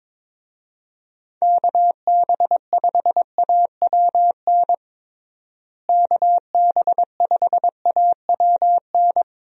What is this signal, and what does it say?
Morse code, heard as a keyed tone, about 22 wpm, 700 Hz. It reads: KB5AWN KB5AWN
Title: USA Call Signs - 6 Characters - Repeated 3x - 22wpm